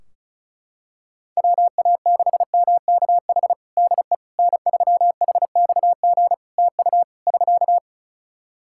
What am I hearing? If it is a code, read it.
Morse code, heard as a keyed tone, about 36 wpm, 700 Hz. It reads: WA6MKH DE N3HXG TU <SK>
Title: Practice QSO - Exchange 4 - Ending - 35wpm